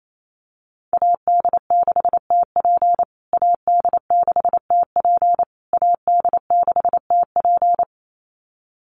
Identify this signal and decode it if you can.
Morse code, heard as a keyed tone, about 28 wpm, 700 Hz. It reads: AD6TP AD6TP AD6TP
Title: USA Call Signs Repeated 3x 28wpm